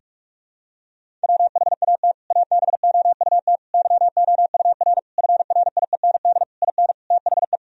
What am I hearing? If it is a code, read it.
Morse code, heard as a keyed tone, about 41 wpm, 700 Hz. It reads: WHAT ABOUT YOUR FRIEND IN THE
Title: Sentences from Top 1000 Words - Review Part 1 - 45wpm